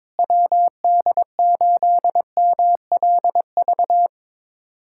Morse code, heard as a keyed tone, about 22 wpm, 700 Hz. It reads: WD8MLV
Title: USA Callsign DE Callsign - 6 Characters - 22wpm